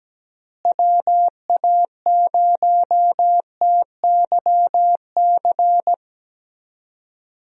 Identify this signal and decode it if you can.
Morse code, heard as a keyed tone, about 17 wpm, 700 Hz. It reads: WA0TYC
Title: USA Call Signs - 6 Characters - Repeated 3x - 17wpm